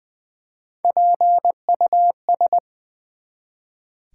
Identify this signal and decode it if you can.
Morse code, heard as a keyed tone, about 20 wpm, 700 Hz. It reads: PUS
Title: Build A Word - 1 Letter At A Time - 4-Letter Words - 20wpm